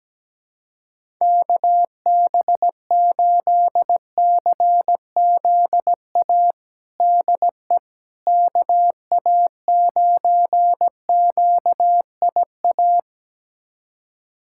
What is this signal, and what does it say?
Morse code, heard as a keyed tone, about 17 wpm, 700 Hz. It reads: KB8CZA DE KA9QIA
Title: USA Callsign DE Callsign 17wpm